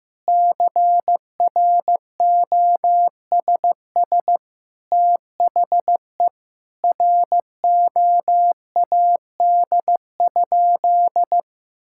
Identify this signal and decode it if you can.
Morse code, heard as a keyed tone, about 15 wpm, 700 Hz. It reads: CROSS THE ROAD?
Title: Sentences from Top 1000 Words - Review Part 5 - 15wpm